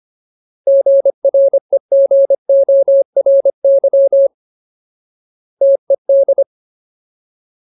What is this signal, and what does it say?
Morse code, heard as a keyed tone, about 25 wpm, 550 Hz. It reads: GREGORY TED
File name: Ultimate Word List / 25WPM Ultimate